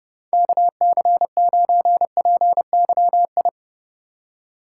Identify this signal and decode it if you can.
Morse code, heard as a keyed tone, about 30 wpm, 700 Hz. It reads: KC9PYI
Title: USA Call Signs - 6 Characters - 3x Character Spacing - 30wpm